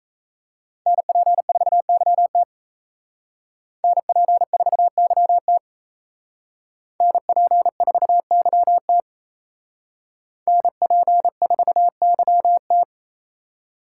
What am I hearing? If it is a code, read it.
Morse code, heard as a keyed tone, about 35 wpm, 700 Hz. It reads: NP4YT NP4YT NP4YT NP4YT